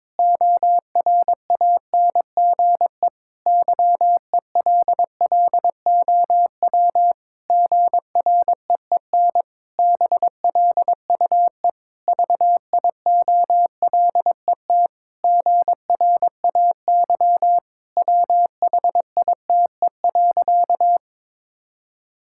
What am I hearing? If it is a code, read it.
Morse code, heard as a keyed tone, about 22 wpm, 700 Hz. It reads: ORANGE YELLOW GREEN BLUE VIOLET GRAY WHITE.
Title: Pangram Challenge v2 - 22wpm